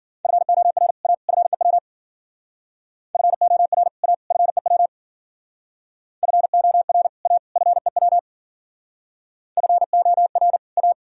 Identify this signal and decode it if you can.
Morse code, heard as a keyed tone, about 53 wpm, 700 Hz. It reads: FOR A FEW FOR A FEW FOR A FEW FOR A